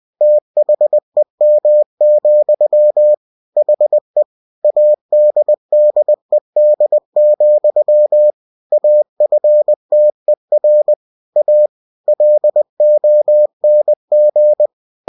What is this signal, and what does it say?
Morse code, heard as a keyed tone, about 20 wpm, 600 Hz. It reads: THEM, HE ADDED, AFTER A LONG
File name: cw-books / winnie-the-pooh / winnie-20wpm-page-0027